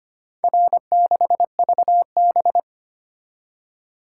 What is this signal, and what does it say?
Morse code, heard as a keyed tone, about 25 wpm, 700 Hz. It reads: R6VB
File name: practice / 25wpm / Four Letter-Number Combinations - 25wpm